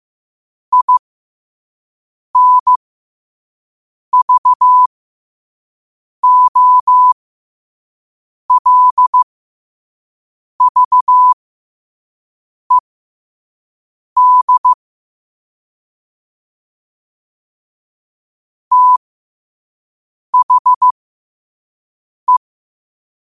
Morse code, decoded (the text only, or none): INVOLVED THE